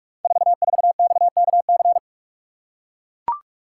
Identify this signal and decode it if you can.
Morse code, heard as a keyed tone, about 44 wpm, 700 Hz. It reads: 44XKC
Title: Code Groups - 5 Letter:Numbers - 45wpm